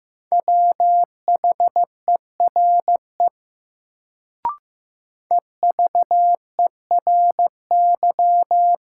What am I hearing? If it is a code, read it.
Morse code, heard as a keyed tone, about 16 wpm, 700 Hz. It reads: WHERE EVERY
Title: Sentences from Top 200 Words - New Word Drill - 15wpm